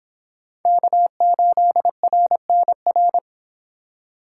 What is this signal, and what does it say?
Morse code, heard as a keyed tone, about 24 wpm, 700 Hz. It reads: K8RNR